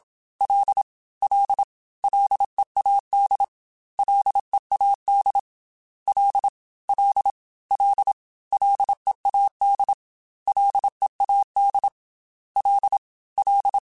Morse code, heard as a keyed tone, about 27 wpm, 800 Hz. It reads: L L LEAD LEAD L L L LEAD LEAD L L
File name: Ls / Single L m3s / Single L @25